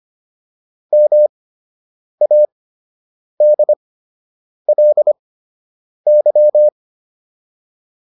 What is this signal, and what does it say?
Morse code, heard as a keyed tone, about 25 wpm, spaced as for 8 wpm, 600 Hz. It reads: MADLY